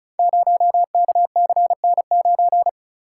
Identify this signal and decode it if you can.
Morse code, heard as a keyed tone, about 36 wpm, 700 Hz. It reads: 0KCN9
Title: Code Groups - 5 Letter:Numbers - 35wpm